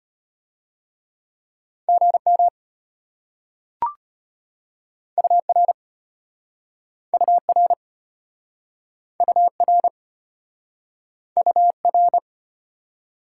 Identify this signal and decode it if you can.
Morse code, heard as a keyed tone, about 33 wpm, 700 Hz. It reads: GM UR UR UR UR